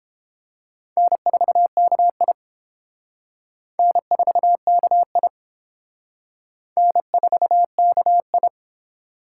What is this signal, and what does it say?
Morse code, heard as a keyed tone, about 29 wpm, 700 Hz. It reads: N4KI N4KI N4KI